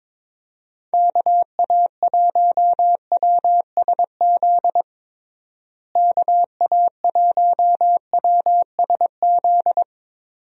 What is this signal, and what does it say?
Morse code, heard as a keyed tone, about 22 wpm, 700 Hz. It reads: KA1WSZ KA1WSZ
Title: USA Call Signs - 6 Characters - Repeated 2x - 22wpm